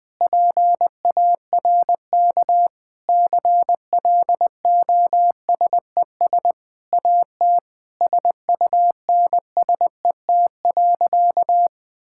Morse code, heard as a keyed tone, about 20 wpm, 700 Hz. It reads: PARK CLOSES AT SUNSET.
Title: Sentences with 6-Word Phrases - Spoken Beginning - 20wpm